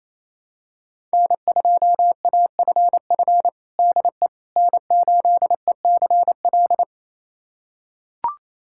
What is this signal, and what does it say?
Morse code, heard as a keyed tone, about 28 wpm, 700 Hz. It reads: N2AFF DE N8ECL
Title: USA Callsign DE Callsign - 5 Characters - 28wpm